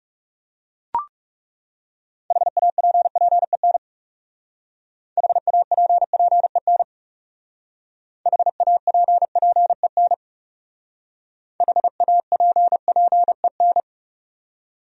Morse code, heard as a keyed tone, about 34 wpm, 700 Hz. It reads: HAPPEN HAPPEN HAPPEN HAPPEN
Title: Speed-Racing Top 500 Words 30wpm